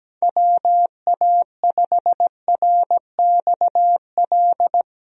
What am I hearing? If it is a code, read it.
Morse code, heard as a keyed tone, about 17 wpm, 700 Hz. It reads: WA5RXL
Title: USA Call Signs - 6 Characters - Repeated 3x - 17wpm